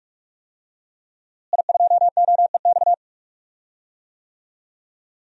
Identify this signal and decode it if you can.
Morse code, heard as a keyed tone, about 44 wpm, 700 Hz. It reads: I2OEX
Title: DXCC-100 Call Signs - 5 Characters - Repeated 2x - 45wpm